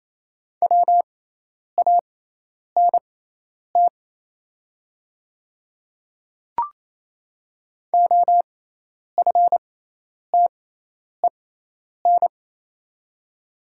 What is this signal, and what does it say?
Morse code, heard as a keyed tone, about 27 wpm, spaced as for 10 wpm, 690 Hz. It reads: WANT OFTEN
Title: 6x Character Spacing - Top 500 Words - 28wpm